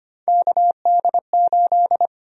Morse code, heard as a keyed tone, about 25 wpm, 700 Hz. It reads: KD8
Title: USA Call Sign Prefixes - Repeated 2x - 25wpm